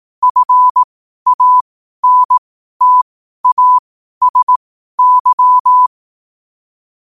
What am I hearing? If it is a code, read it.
Morse code, heard as a keyed tone, about 18 wpm, spaced as for 12 wpm, 1000 Hz. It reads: FANTASY
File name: qotdfiles / 12wpm